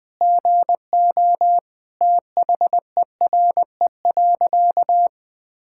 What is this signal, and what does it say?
Morse code, heard as a keyed tone, about 20 wpm, 700 Hz. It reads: GO THERE.
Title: Sentences with 5-Word Phrases - Spoken Beginning - 20wpm